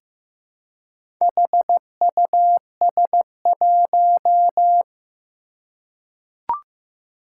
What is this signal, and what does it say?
Morse code, heard as a keyed tone, about 15 wpm, 700 Hz. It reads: HUS1
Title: Four Challenging Letter-Number Combinations 15wpm